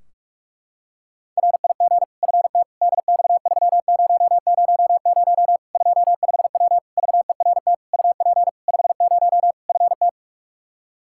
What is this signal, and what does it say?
Morse code, heard as a keyed tone, about 46 wpm, 700 Hz. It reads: RIG FT DX3000 25W VERT UP 50 FT
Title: Practice QSO - Exchange 2 - Info - 45wpm